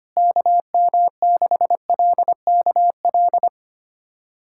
Morse code, heard as a keyed tone, about 25 wpm, 700 Hz. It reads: KM6LKL
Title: USA Callsign DE Callsign - 6 Characters - 25wpm